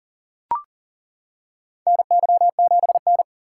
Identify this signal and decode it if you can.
Morse code, heard as a keyed tone, about 40 wpm, 700 Hz. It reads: NY7N
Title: K1USN SST QSO Practice - 40wpm